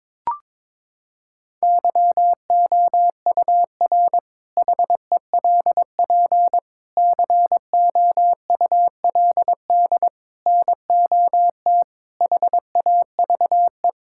Morse code, heard as a keyed tone, about 22 wpm, 700 Hz. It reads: YOUR HELP COULD NOT HAVE
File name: practice / 22wpm / Sentences with 8-Word Phrases - 22wpm